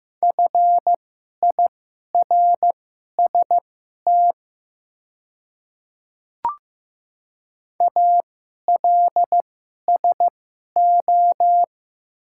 Morse code, heard as a keyed tone, about 15 wpm, spaced as for 11 wpm, 700 Hz. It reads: FIRST ALSO
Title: 2x Character Spacing - Top 200 Words - 15wpm